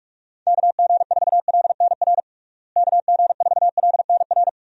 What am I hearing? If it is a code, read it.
Morse code, heard as a keyed tone, about 46 wpm, 700 Hz. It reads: KG4LNR KG4LNR
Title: USA Call Signs - 6 Characters - Repeated 3x - 45wpm